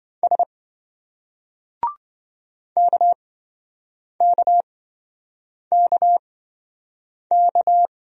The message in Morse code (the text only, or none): S KKKK